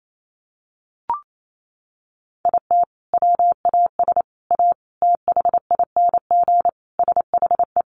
Morse code, heard as a keyed tone, about 28 wpm, 700 Hz. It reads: IT WAS A THING SHE
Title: Sentences from Top 1400 Words 28wpm